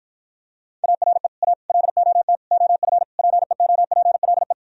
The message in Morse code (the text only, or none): ARE A LOT OF PEOPLE